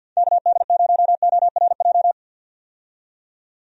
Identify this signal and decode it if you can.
Morse code, heard as a keyed tone, about 51 wpm, 700 Hz. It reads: KD0ORJ
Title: USA Call Signs - 6 Characters - 50wpm